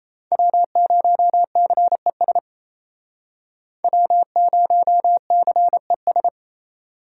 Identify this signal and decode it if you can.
Morse code, heard as a keyed tone, about 35 wpm, 700 Hz. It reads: W0CES W0CES
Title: Speed Racing - USA 1x3 Call Signs 28wpm